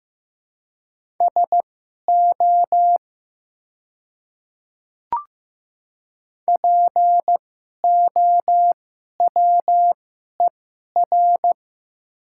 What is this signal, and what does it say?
Morse code, heard as a keyed tone, about 15 wpm, spaced as for 11 wpm, 700 Hz. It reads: SO POWER
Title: 2x Character Spacing - Top 500 Words - 15wpm